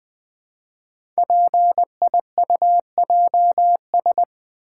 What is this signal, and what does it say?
Morse code, heard as a keyed tone, about 20 wpm, 700 Hz. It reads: PIUJS
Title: Code Groups - 5 Letters - 20wpm